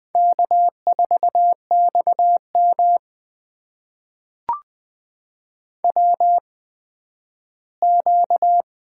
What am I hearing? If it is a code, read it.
Morse code, heard as a keyed tone, about 20 wpm, spaced as for 12 wpm, 700 Hz. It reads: K4XM WQ
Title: USA Call Signs - 4 Characters - 8x Character Spacing - 20wpm